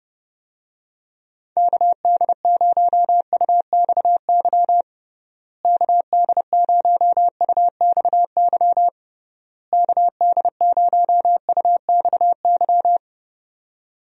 Morse code, heard as a keyed tone, about 30 wpm, 700 Hz. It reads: KD0UXY KD0UXY KD0UXY
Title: USA Call Signs - 6 Characters - Repeated 3x - 30wpm